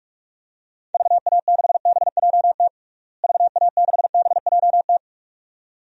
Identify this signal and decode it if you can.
Morse code, heard as a keyed tone, about 46 wpm, 700 Hz. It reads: VA6BJT VA6BJT
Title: DXCC-100 Call Signs - 6 Characters - Repeated 2x - 45wpm